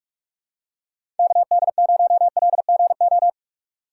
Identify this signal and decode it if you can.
Morse code, heard as a keyed tone, about 46 wpm, 700 Hz. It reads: KD0LGO
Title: USA Call Signs - 6 Characters - 4x Character Spacing - 45wpm